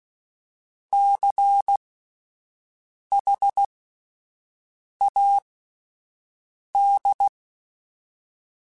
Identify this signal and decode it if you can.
Morse code, heard as a keyed tone, about 16 wpm, spaced as for 5 wpm, 800 Hz. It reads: CHAD
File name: Alphabetical / Alphabetical mp3s / Cs @5